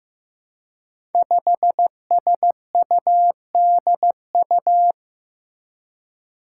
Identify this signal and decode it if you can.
Morse code, heard as a keyed tone, about 15 wpm, 700 Hz. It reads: 5SUDU
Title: Five Challenging Letter-Number Combinations 15wpm